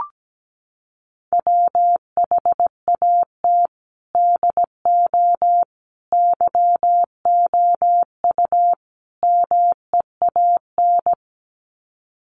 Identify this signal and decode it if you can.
Morse code, heard as a keyed tone, about 17 wpm, 700 Hz. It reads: WHAT DO YOU MEAN
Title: Sets of 4 Words 17wpm